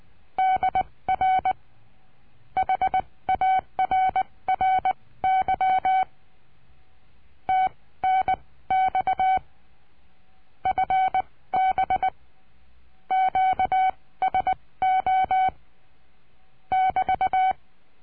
Morse code, heard as a keyed tone, about 20 wpm, spaced as for 16 wpm, 760 Hz. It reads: DR HARRY TNX FB QSO <BT>
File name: cursos / CURSO DE TELEGRAFIA / Telegrafia - Trilha 22